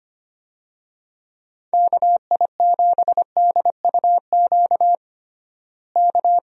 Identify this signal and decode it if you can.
Morse code, heard as a keyed tone, about 25 wpm, 700 Hz. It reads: KI7DUQ K
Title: USA Call Signs - 6 Characters - Repeated 3x - 25wpm